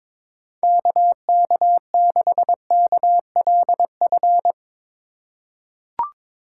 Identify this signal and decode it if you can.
Morse code, heard as a keyed tone, about 21 wpm, 700 Hz. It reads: KK6KLF
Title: USA Call Signs - 6 Characters - 2x Character Spacing - 22wpm